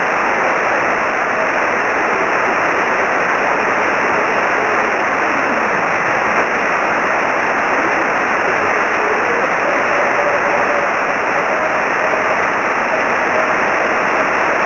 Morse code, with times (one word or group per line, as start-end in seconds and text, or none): none